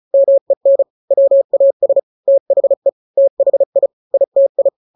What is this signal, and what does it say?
Morse code, heard as a keyed tone, about 34 wpm, 550 Hz. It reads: MEN WAS THE THI ITI
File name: 3-grams 35 30 WPM